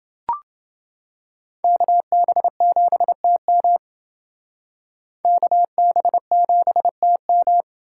Morse code, heard as a keyed tone, about 29 wpm, 700 Hz. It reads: KB7TM KB7TM